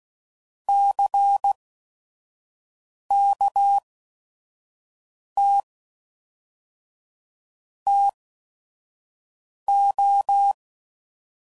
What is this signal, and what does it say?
Morse code, heard as a keyed tone, about 16 wpm, spaced as for 5 wpm, 800 Hz. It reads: CKT TO
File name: CW Code Groups @5WPM